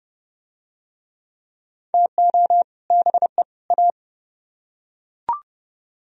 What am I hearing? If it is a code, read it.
Morse code, heard as a keyed tone, about 30 wpm, 700 Hz. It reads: TO BE A